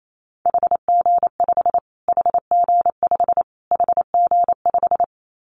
Morse code, heard as a keyed tone, about 28 wpm, 700 Hz. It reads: HG5 HG5 HG5
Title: Callsign Prefixes for Top 25 DXCC - Repeated 3x - 28wpm